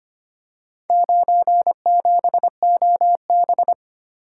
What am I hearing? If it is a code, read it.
Morse code, heard as a keyed tone, about 25 wpm, 700 Hz. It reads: 97OB
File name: practice / 25wpm / Four Letter-Number Combinations - 25wpm